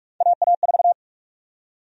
Morse code, heard as a keyed tone, about 46 wpm, 700 Hz. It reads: AA4
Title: USA Call Sign Prefixes - 45wpm